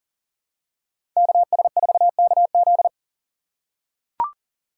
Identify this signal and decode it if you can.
Morse code, heard as a keyed tone, about 40 wpm, 700 Hz. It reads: KS4KZ
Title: USA Call Signs - 5 Characters - 6x Character Spacing - 40wpm